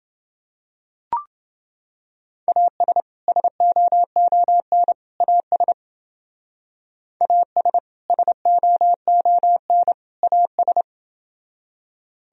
Morse code, heard as a keyed tone, about 29 wpm, 700 Hz. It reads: AS SOON AS AS SOON AS